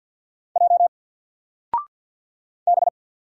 Morse code, heard as a keyed tone, about 50 wpm, 700 Hz. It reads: J B
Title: Single Letter-Number 50wpm